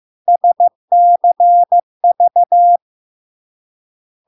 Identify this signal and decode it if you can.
Morse code, heard as a keyed tone, about 15 wpm, 700 Hz. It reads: SCV
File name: LARK-FD-2022-Log-in-CW-15wpm-700-Hz